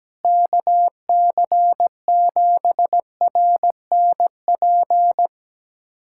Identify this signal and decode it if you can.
Morse code, heard as a keyed tone, about 17 wpm, 700 Hz. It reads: KC7RNP
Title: USA Callsign DE Callsign - 6 Characters - 17wpm